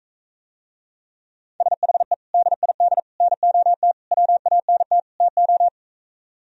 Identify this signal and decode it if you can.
Morse code, heard as a keyed tone, about 43 wpm, 700 Hz. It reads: SHE DID NOT WANT TO